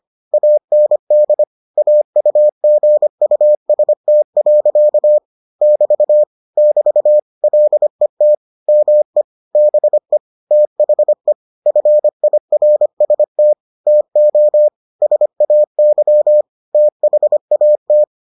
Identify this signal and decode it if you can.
Morse code, heard as a keyed tone, about 25 wpm, 600 Hz. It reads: AND AUGUST. <BT> <BT> LET ME BE THE FIRST TO SAY THAT